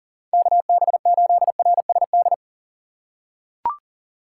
Morse code, heard as a keyed tone, about 39 wpm, 700 Hz. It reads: KB8RSD
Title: USA Call Signs - 6 Characters - 3x Character Spacing - 40wpm